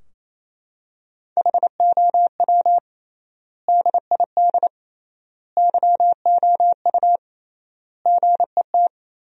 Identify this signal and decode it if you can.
Morse code, heard as a keyed tone, about 28 wpm, 700 Hz. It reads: HOW DID YOU GET
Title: 3x Word-Spacing - Sentences from Top 200 Words - 28wpm